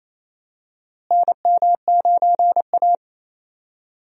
Morse code, heard as a keyed tone, about 28 wpm, 700 Hz. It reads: NM9A
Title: USA Call Signs - 4 Characters - Repeated 3x - 28wpm